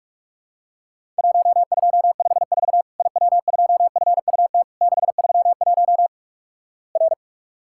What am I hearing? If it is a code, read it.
Morse code, heard as a keyed tone, about 44 wpm, 700 Hz. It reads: 1254 IW2FUT 631 R
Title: Worked All Europe DX Contest - QTC Exchange - 45wpm